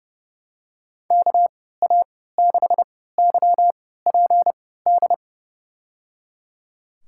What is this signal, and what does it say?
Morse code, heard as a keyed tone, about 30 wpm, spaced as for 17 wpm, 700 Hz. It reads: KA6YPD
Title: USA Call Signs - 3x Character Spacing - 30wpm